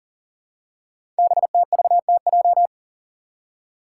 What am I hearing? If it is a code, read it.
Morse code, heard as a keyed tone, about 44 wpm, 690 Hz. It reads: BTVTJ
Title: Code Groups - 5 Letters - 40wpm